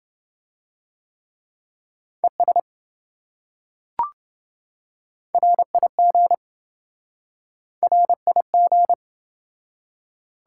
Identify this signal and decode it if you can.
Morse code, heard as a keyed tone, about 26 wpm, 700 Hz. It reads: ES RIG RIG